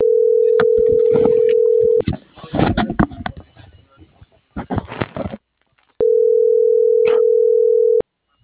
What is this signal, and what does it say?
Ringback tone (North America), 440 + 480 Hz.